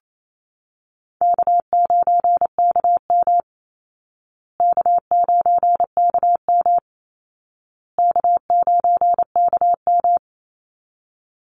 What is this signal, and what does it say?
Morse code, heard as a keyed tone, about 28 wpm, 700 Hz. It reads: K9KM K9KM K9KM
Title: Easy Call Signs Repeated 3x - 28wpm